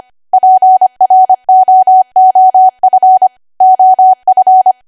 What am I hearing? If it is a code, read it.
Morse code, heard as a keyed tone, about 25 wpm, 750 Hz. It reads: PROOF OF
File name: Archive / 25 WPM / 25WPM